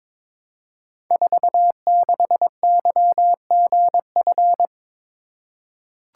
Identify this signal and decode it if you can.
Morse code, heard as a keyed tone, about 22 wpm, 700 Hz. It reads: 46YGF
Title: Code Groups - 5 Letter:Numbers - 22wpm